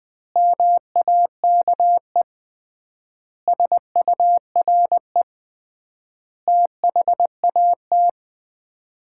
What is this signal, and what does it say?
Morse code, heard as a keyed tone, about 20 wpm, 700 Hz. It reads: MAKE SURE THAT
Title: 3x Word-Spacing - Sets of 5 Words - 20wpm